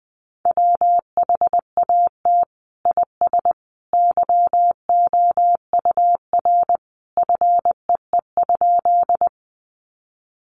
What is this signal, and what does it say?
Morse code, heard as a keyed tone, about 20 wpm, 700 Hz. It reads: WHAT IS YOUR FEE?
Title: Sentences from Top 1500 Words 20wpm